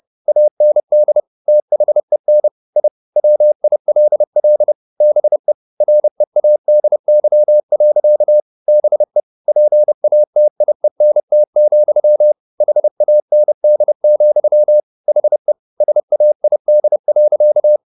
Morse code, heard as a keyed tone, about 30 wpm, 600 Hz. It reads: AND THEN I WILL BE READY. BE PATIENT, HAND, HE SAID.
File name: the old man and the sea / the old man and the sea 1 30wpm